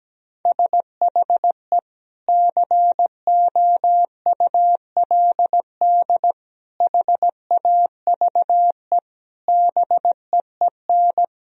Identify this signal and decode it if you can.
Morse code, heard as a keyed tone, about 17 wpm, 700 Hz. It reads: SHE COULD HAVE BEEN
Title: Sentences from Top 1000 Words - Review Part 1 - 17wpm